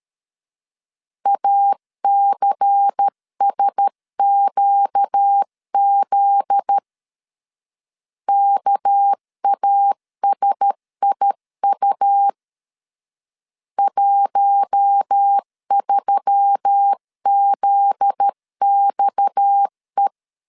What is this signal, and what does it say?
Morse code, heard as a keyed tone, about 13 wpm, 800 Hz. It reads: ACSQZ KASIU 13ZXE
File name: LEZIONI / 7 GF9Z / 7 GF9Z 12wpm 2147